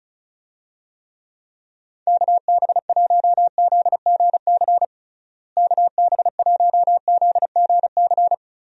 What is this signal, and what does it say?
Morse code, heard as a keyed tone, about 34 wpm, 700 Hz. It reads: KB1ZGC KB1ZGC
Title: USA Call Signs - 6 Characters - Repeated 2x - 35wpm